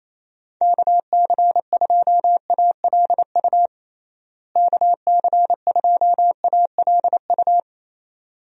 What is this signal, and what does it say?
Morse code, heard as a keyed tone, about 28 wpm, 700 Hz. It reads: KC2ALU KC2ALU
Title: USA Call Signs - 6 Characters - Repeated 2x - 28wpm